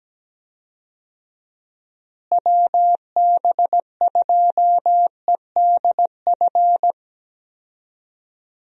Morse code, heard as a keyed tone, about 17 wpm, 700 Hz. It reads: WB2EDF
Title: USA Call Signs - 6 Characters - Repeated 2x - 17wpm